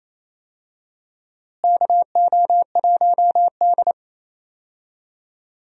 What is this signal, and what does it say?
Morse code, heard as a keyed tone, about 28 wpm, 700 Hz. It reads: KO1D
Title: USA Call Signs - 4 Characters - Repeated 2x - 28wpm